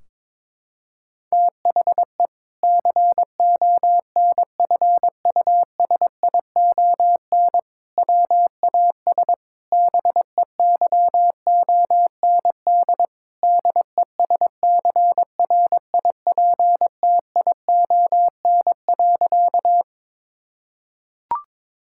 Morse code, heard as a keyed tone, about 21 wpm, 700 Hz. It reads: THE CONFUSION WAS BEYOND DESCRIPTION.
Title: Sentences from Top 1600 Words 22wpm